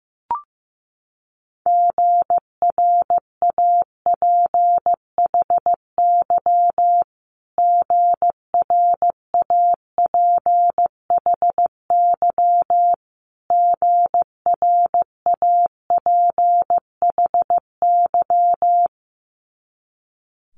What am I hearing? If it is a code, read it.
Morse code, heard as a keyed tone, about 15 wpm, 700 Hz. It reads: GRAPHY GRAPHY GRAPHY
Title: Multi-letter Phonemes Repeated 3x 15wpm